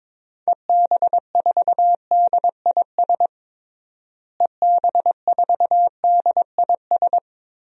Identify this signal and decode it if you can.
Morse code, heard as a keyed tone, about 22 wpm, 700 Hz. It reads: EB4DIS EB4DIS
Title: DXCC-100 Call Signs - 6 Characters - Repeated 3x - 22wpm